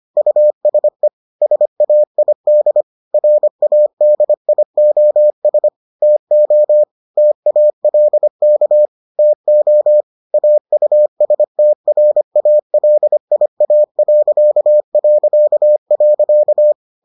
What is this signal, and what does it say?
Morse code, heard as a keyed tone, about 25 wpm, 600 Hz. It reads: USE SAID RADIOS TO TALK TO AUSTRALIA...